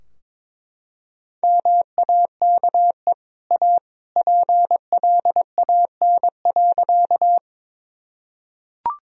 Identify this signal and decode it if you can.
Morse code, heard as a keyed tone, about 21 wpm, 700 Hz. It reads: MAKE A PLAN.
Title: Sentences from Top 1000 Words - Review Part 2 - 22wpm